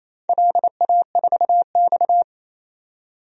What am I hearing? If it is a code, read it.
Morse code, heard as a keyed tone, about 28 wpm, 700 Hz. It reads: LA4X
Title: DXCC-100 Call Signs - 4 Characters - 28wpm